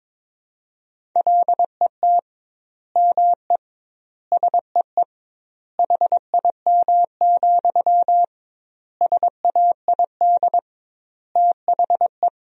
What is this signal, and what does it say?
Morse code, heard as a keyed tone, about 22 wpm, 700 Hz. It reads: LET ME SEE HIM, SAID THE
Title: 2x Word-Spacing - Sentences from Top 400 Words - 22wpm